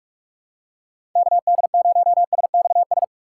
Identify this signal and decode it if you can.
Morse code, heard as a keyed tone, about 44 wpm, 700 Hz. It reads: KD0SXS
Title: USA Call Signs - 6 Characters - 4x Character Spacing - 45wpm